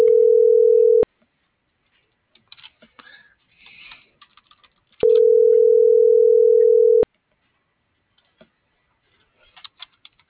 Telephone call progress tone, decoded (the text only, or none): ringback tone